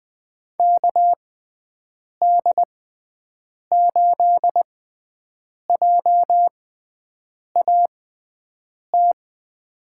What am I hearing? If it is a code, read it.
Morse code, heard as a keyed tone, about 20 wpm, spaced as for 7 wpm, 700 Hz. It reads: KD8JAT
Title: USA Call Signs - 6 Characters - 6x Character Spacing - 20wpm